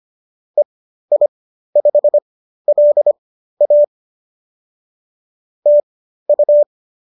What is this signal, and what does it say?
Morse code, heard as a keyed tone, about 25 wpm, spaced as for 13 wpm, 590 Hz. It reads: EI5LA TU